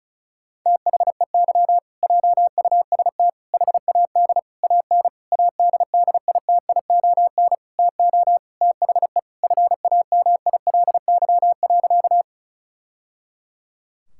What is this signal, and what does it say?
Morse code, heard as a keyed tone, about 36 wpm, 690 Hz. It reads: THEY JUST HAD AN ADDITION TO THE FAMILY.
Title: Sentences from Top 1000 Words - Review Part 6 - 35wpm